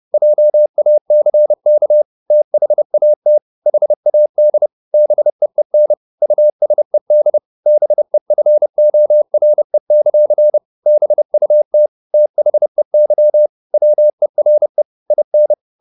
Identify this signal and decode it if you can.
Morse code, heard as a keyed tone, about 30 wpm, 600 Hz. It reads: JACK THAT HAD BEEN USED BEFORE; BUT THEY WERE IN
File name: the old man and the sea / the old man and the sea 1 30wpm